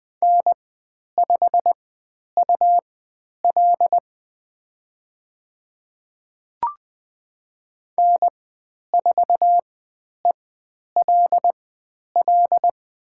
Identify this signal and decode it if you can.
Morse code, heard as a keyed tone, about 20 wpm, spaced as for 10 wpm, 700 Hz. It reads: N5UL N4ELL
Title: 102 Lesson 17g - Letter u - Call Signs 20wpm